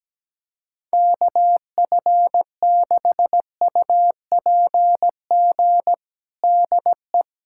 Morse code, heard as a keyed tone, about 17 wpm, 700 Hz. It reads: KF6UPG DE
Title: USA Callsign DE Callsign - 6 Characters - 17wpm